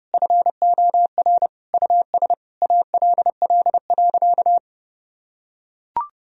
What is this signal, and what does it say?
Morse code, heard as a keyed tone, about 30 wpm, 700 Hz. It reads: FOR US ALL.
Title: Sentences from Top 1000 Words - Review Part 1 - 30wpm